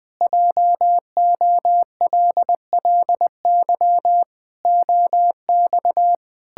Morse code, heard as a keyed tone, about 20 wpm, 700 Hz. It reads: JOLLY OX
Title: Pangram Challenge v2 - 20wpm